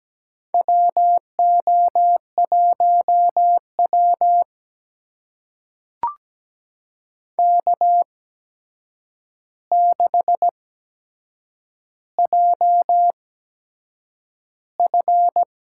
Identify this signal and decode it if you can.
Morse code, heard as a keyed tone, about 17 wpm, spaced as for 7 wpm, 700 Hz. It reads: WO1W K6JF
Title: USA Call Signs - 4 Characters - 8x Character Spacing - 17wpm